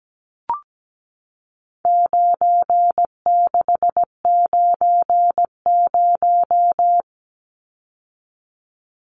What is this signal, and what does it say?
Morse code, heard as a keyed tone, about 17 wpm, 700 Hz. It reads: 9690
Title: Numbers - 4-Digit 17wpm